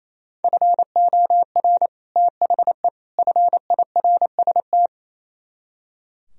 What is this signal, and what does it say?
Morse code, heard as a keyed tone, about 36 wpm, 700 Hz. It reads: FOR THE FIRST